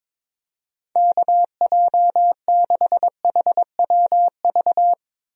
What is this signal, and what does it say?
Morse code, heard as a keyed tone, about 22 wpm, 700 Hz. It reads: KJ6HWV
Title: USA Call Signs - 6 Characters - Repeated 2x - 22wpm